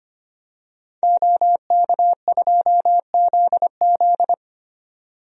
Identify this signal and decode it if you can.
Morse code, heard as a keyed tone, about 25 wpm, 700 Hz. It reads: OK2ZZ
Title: DXCC-100 Call Signs - 5 Characters - 25wpm